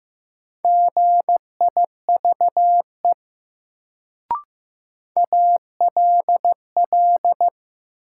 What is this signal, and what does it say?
Morse code, heard as a keyed tone, about 16 wpm, 700 Hz. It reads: GIVE ALL
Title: Sets of 2 Words - New Word Drill - 15wpm